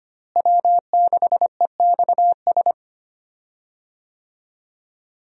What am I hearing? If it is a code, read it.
Morse code, heard as a keyed tone, about 25 wpm, 700 Hz. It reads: W6EXS
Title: USA Call Signs - 5 Characters - Repeated 3x - 25wpm